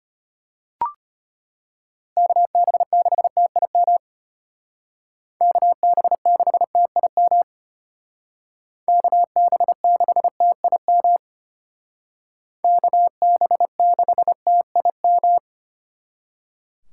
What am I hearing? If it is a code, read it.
Morse code, heard as a keyed tone, about 33 wpm, 700 Hz. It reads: KB6TIM KB6TIM KB6TIM KB6TIM